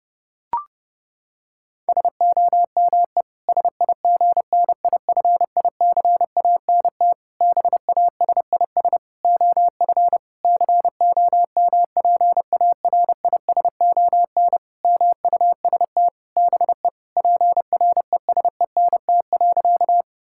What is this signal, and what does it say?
Morse code, heard as a keyed tone, about 30 wpm, 700 Hz. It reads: SOME SIGNIFICANT BASIS OF COMPARISON MUST BE PRESENT.
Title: Sentences from Top 1600 Words 30wpm